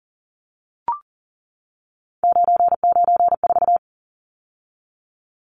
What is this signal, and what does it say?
Morse code, heard as a keyed tone, about 40 wpm, 700 Hz. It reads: 994
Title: Numbers - 3-Digit 40wpm